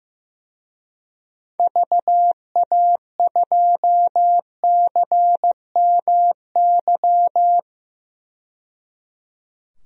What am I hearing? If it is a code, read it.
Morse code, heard as a keyed tone, about 16 wpm, 700 Hz. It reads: VA2CMY
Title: DXCC-100 Call Signs - 6 Characters - 15wpm